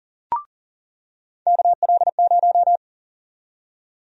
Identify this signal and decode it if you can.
Morse code, heard as a keyed tone, about 40 wpm, 700 Hz. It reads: KL0
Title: USA Call Sign Prefixes - 40wpm